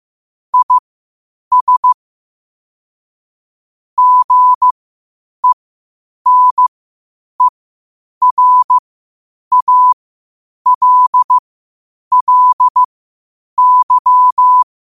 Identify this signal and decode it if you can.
Morse code, heard as a keyed tone, about 15 wpm, spaced as for 8 wpm, 1000 Hz. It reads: IS GENERALLY
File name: qotdfiles / 8wpm